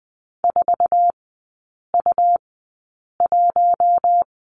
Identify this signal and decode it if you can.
Morse code, heard as a keyed tone, about 20 wpm, spaced as for 8 wpm, 700 Hz. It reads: 4U1
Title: 121 Lesson 20e - Number 3 - Three Characters 20wpm